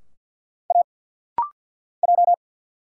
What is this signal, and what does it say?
Morse code, heard as a keyed tone, about 50 wpm, 700 Hz. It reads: A J
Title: Single Letters - Rapid-Fire 50wpm